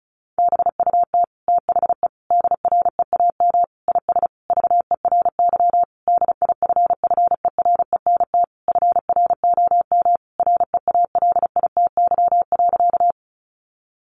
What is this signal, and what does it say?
Morse code, heard as a keyed tone, about 34 wpm, 700 Hz. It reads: BUT THE DREAM IS VERY DIFFERENT FROM REALITY.
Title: Sentences from Top 1500 Words 35wpm